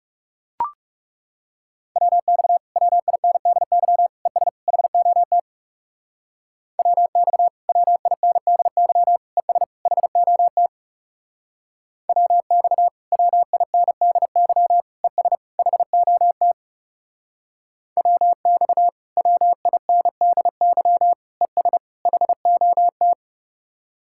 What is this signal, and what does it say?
Morse code, heard as a keyed tone, about 35 wpm, 700 Hz. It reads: WX WINDY ES HOT WX WINDY ES HOT WX WINDY ES HOT WX WINDY ES HOT
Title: Speed-Racing Common QSO Phrases - 30wpm